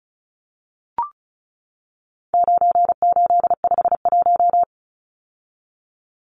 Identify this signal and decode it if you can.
Morse code, heard as a keyed tone, about 34 wpm, 700 Hz. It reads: 9851
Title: Numbers - 4-Digit 35wpm